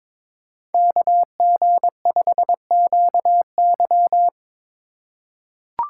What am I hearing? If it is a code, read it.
Morse code, heard as a keyed tone, about 23 wpm, 700 Hz. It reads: KG5QY
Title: USA Call Signs - 5 Characters - 6x Character Spacing - 22wpm